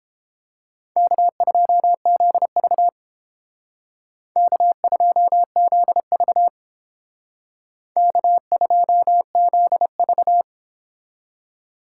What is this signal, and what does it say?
Morse code, heard as a keyed tone, about 30 wpm, 700 Hz. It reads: K2ZV K2ZV K2ZV